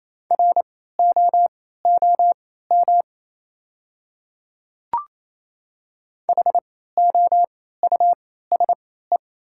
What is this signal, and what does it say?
Morse code, heard as a keyed tone, about 28 wpm, spaced as for 16 wpm, 700 Hz. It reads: ROOM HOUSE
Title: 3x Character Spacing - Top 300 Words - 28wpm